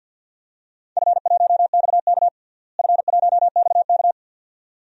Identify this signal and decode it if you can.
Morse code, heard as a keyed tone, about 46 wpm, 700 Hz. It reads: F1XK F1XK
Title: DXCC-100 Call Signs - 4 Characters - Repeated 2x - 50wpm